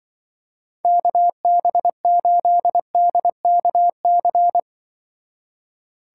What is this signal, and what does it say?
Morse code, heard as a keyed tone, about 24 wpm, 700 Hz. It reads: KB8DKC
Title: Speed Racing - USA 2x3 Call Signs 20wpm